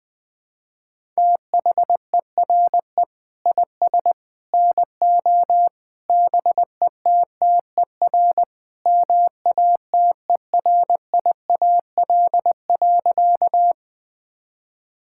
Morse code, heard as a keyed tone, about 20 wpm, 700 Hz. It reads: THERE IS NO BETTER MATERIAL.
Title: Sentences from Top 1000 Words - Review Part 5 - 20wpm